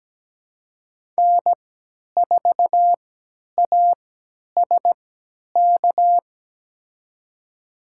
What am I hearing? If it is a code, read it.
Morse code, heard as a keyed tone, about 17 wpm, spaced as for 10 wpm, 700 Hz. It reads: N4ASK
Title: USA Call Signs - 5 Characters - 3x Character Spacing - 17wpm